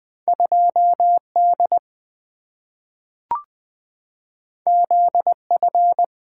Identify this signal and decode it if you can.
Morse code, heard as a keyed tone, about 20 wpm, 700 Hz. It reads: 2D ZF
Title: Two Letter-Number Combinations - 20wpm